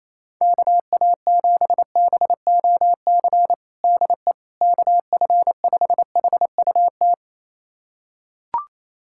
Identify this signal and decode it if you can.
Morse code, heard as a keyed tone, about 28 wpm, 700 Hz. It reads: KA7BOC DE KF5HUT
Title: USA Callsign DE Callsign - 6 Characters - 28wpm